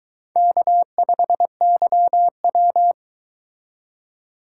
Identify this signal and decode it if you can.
Morse code, heard as a keyed tone, about 22 wpm, 700 Hz. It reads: K5YW